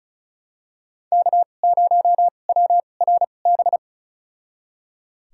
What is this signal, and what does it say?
Morse code, heard as a keyed tone, about 34 wpm, spaced as for 25 wpm, 700 Hz. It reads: K0WRB
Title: USA Call Signs - 2x Character Spacing - 35wpm